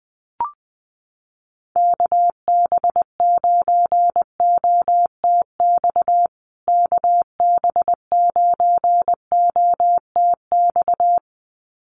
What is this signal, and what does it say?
Morse code, heard as a keyed tone, about 20 wpm, 700 Hz. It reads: KB9OTX KB9OTX
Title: USA Call Signs Repeated 2x 20wpm